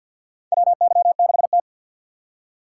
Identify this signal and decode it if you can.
Morse code, heard as a keyed tone, about 50 wpm, 700 Hz. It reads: WY6T
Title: USA Call Signs - 4 Characters - 6x Character Spacing - 50wpm